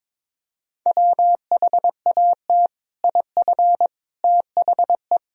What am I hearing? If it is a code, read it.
Morse code, heard as a keyed tone, about 21 wpm, 700 Hz. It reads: WHAT IF THE
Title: Sentences from Top 1000 Words - Review Part 2 - 22wpm